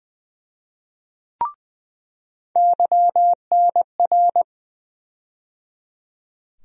Morse code, heard as a keyed tone, about 20 wpm, 700 Hz. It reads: YNR
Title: USA Call Sign Postfixes - 3 Letters 20wpm